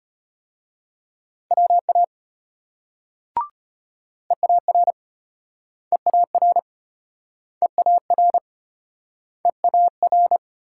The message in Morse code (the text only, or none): WA EAR EAR EAR EAR